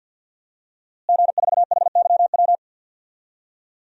Morse code, heard as a keyed tone, about 47 wpm, 700 Hz. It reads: G4HYW
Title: DXCC-100 Call Signs - 5 Characters - Repeated 3x - 50wpm